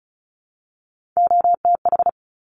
Morse code, heard as a keyed tone, about 36 wpm, 700 Hz. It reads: OTH
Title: Top Three Letter Combinations in Words 35wpm